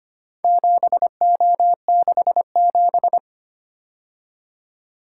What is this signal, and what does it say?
Morse code, heard as a keyed tone, about 25 wpm, 700 Hz. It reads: 7O67
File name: practice / 25wpm / Four Letter-Number Combinations - 25wpm